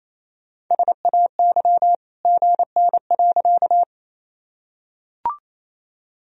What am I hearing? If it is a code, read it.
Morse code, heard as a keyed tone, about 28 wpm, 700 Hz. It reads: SAY GN.
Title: Sentences from Top 300 Words and Abbreviations 28wpm